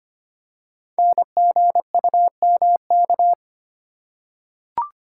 Morse code, heard as a keyed tone, about 25 wpm, 700 Hz. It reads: NGUMK
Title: Code Groups - 5 Letters - 25wpm